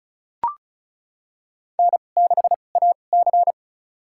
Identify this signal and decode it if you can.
Morse code, heard as a keyed tone, about 36 wpm, spaced as for 25 wpm, 700 Hz. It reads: N6AC
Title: USA Call Signs - 4 Characters - 2x Character Spacing - 35wpm